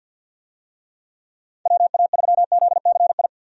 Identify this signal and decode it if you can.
Morse code, heard as a keyed tone, about 49 wpm, 700 Hz. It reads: WA3ZCI
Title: USA Call Signs - 6 Characters - 50wpm